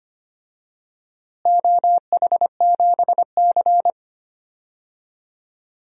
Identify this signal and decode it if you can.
Morse code, heard as a keyed tone, about 25 wpm, 700 Hz. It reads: OH7C
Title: DXCC-100 Call Signs - 4 Characters - Repeated 2x - 25wpm